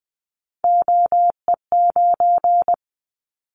Morse code, heard as a keyed tone, about 20 wpm, 700 Hz. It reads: OE9
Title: Callsign Prefixes for Top 100 DXCC - 20wpm